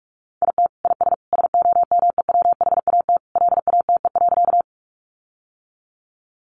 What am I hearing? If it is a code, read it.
Morse code, heard as a keyed tone, about 44 wpm, 700 Hz. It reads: IT IS SOMEWHAT LATE.
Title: Sentences from Top 1300 Words 45wpm